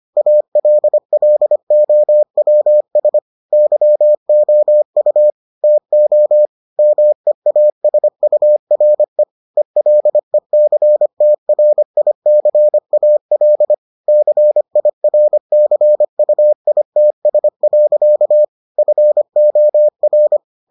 Morse code, heard as a keyed tone, about 25 wpm, 600 Hz. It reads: ALLOWS YOU TO MEASURE ELECTRICAL CIRCUITS. FOR